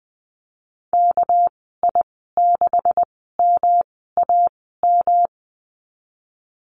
Morse code, heard as a keyed tone, about 20 wpm, spaced as for 15 wpm, 700 Hz. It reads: KI6MAM
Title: 207 Lesson 33g - Letter j - Call Signs 20wpm